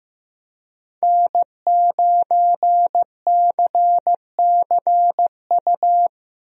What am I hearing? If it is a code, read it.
Morse code, heard as a keyed tone, about 15 wpm, 700 Hz. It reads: N9CCU
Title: USA Call Signs - 5 Characters - Repeated 3x - 15wpm